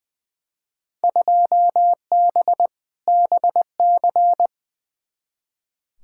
Morse code, heard as a keyed tone, about 20 wpm, 700 Hz. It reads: 2B BC
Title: Field Day Exchanges Encore - 20wpm